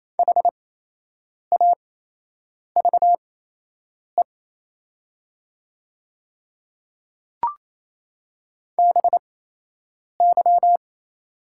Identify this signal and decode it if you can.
Morse code, heard as a keyed tone, about 28 wpm, spaced as for 8 wpm, 690 Hz. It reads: HAVE BY
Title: 8x Character Spacing - Top 100 Words - 28wpm